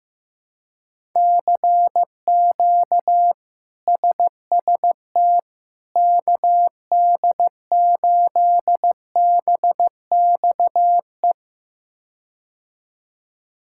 Morse code, heard as a keyed tone, about 15 wpm, 700 Hz. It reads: CQ SST KD8BXE
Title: K1USN SST QSO Practice - 15wpm